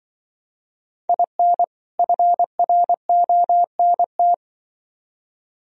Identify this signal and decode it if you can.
Morse code, heard as a keyed tone, about 24 wpm, 700 Hz. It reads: IN FRONT
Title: Speed-Racing - Sets of 2 Words 20wpm